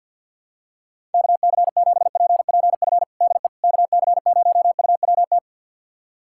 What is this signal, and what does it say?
Morse code, heard as a keyed tone, about 46 wpm, 700 Hz. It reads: KC7PPF DE KC0UWT
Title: USA Callsign DE Callsign - 6 Characters - 50wpm